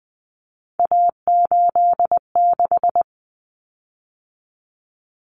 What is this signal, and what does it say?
Morse code, heard as a keyed tone, about 20 wpm, 700 Hz. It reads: A86
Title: Common Cut Numbers - 3-Digit 20wpm